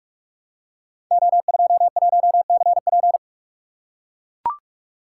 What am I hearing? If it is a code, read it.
Morse code, heard as a keyed tone, about 44 wpm, 700 Hz. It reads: O21CP
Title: Code Groups - 5 Letter:Numbers - 45wpm